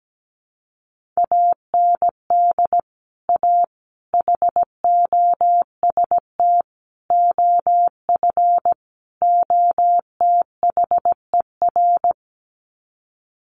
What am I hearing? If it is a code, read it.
Morse code, heard as a keyed tone, about 17 wpm, 700 Hz. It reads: AND A HOST OF OTHER
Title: Sets of 5 Words Encore2 17wpm